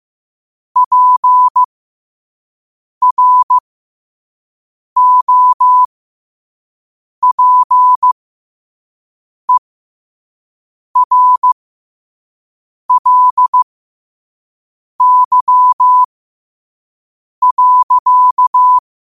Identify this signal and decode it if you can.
Morse code, heard as a keyed tone, about 15 wpm, spaced as for 5 wpm, 1000 Hz. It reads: PROPERLY.
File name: qotdfiles / 5wpm